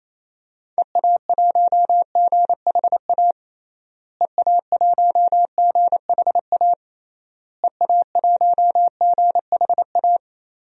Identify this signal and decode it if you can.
Morse code, heard as a keyed tone, about 28 wpm, 700 Hz. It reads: EA1GHA EA1GHA EA1GHA
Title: DXCC-100 Call Signs - 6 Characters - Repeated 3x - 28wpm